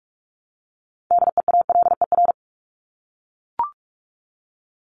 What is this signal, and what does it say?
Morse code, heard as a keyed tone, about 44 wpm, 700 Hz. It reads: DEALER
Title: Top 1001-1250 Words 45wpm